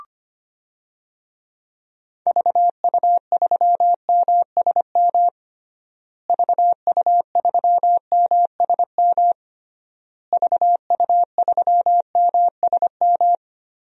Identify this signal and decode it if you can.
Morse code, heard as a keyed tone, about 25 wpm, 700 Hz. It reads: VU3MSM VU3MSM VU3MSM
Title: DXCC-100 Call Signs - 6 Characters - Repeated 3x - 25wpm